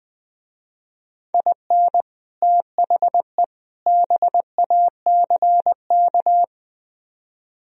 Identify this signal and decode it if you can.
Morse code, heard as a keyed tone, about 20 wpm, 700 Hz. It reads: IN THE BACK